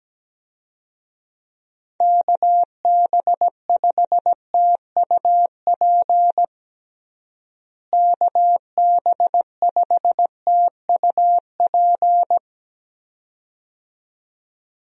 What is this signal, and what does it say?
Morse code, heard as a keyed tone, about 17 wpm, 700 Hz. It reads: KB5TUP KB5TUP
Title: USA Call Signs - 6 Characters - Repeated 2x - 17wpm